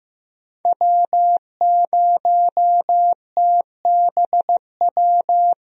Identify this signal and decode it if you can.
Morse code, heard as a keyed tone, about 15 wpm, 700 Hz. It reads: W0TBW
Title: USA Call Signs - 5 Characters - Repeated 3x - 15wpm